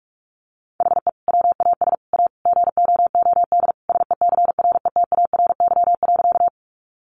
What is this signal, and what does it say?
Morse code, heard as a keyed tone, about 44 wpm, 700 Hz. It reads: HE WAS A GOOD SECRETARY.
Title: Sentences from Top 1300 Words 45wpm